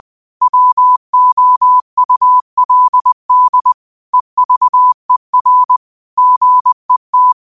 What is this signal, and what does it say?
Morse code, heard as a keyed tone, about 20 wpm, 1000 Hz. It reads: WOULD EVER GET
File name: qotdfiles / 20wpm